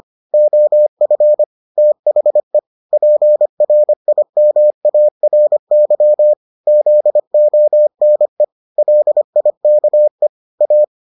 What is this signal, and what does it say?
Morse code, heard as a keyed tone, about 25 wpm, 600 Hz. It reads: OF THE PRIMARY ZONE LIKE A